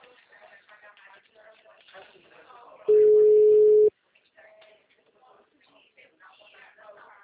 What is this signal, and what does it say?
Ringback tone (Europe, 425 Hz), with speech.